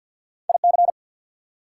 Morse code, heard as a keyed tone, about 47 wpm, 690 Hz. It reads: IC
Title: Prefixes and Suffixes 50wpm